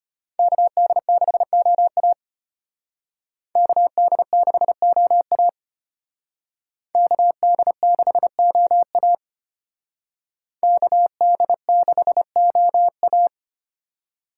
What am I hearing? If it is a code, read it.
Morse code, heard as a keyed tone, about 32 wpm, 700 Hz. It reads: KD6OA KD6OA KD6OA KD6OA